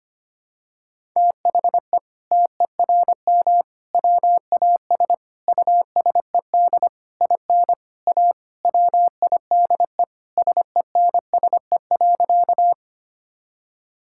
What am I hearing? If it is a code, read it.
Morse code, heard as a keyed tone, about 25 wpm, 700 Hz. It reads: THE TERM WAS USED IN A WIDE SENSE.
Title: Sentences from Top 1000 Words - Review Part 6 - 25wpm